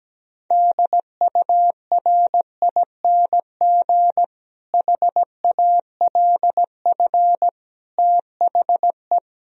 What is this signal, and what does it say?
Morse code, heard as a keyed tone, about 17 wpm, 700 Hz. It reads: DURING HALF THE
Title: Sentences from Top 1000 Words - Review Part 2 - 17wpm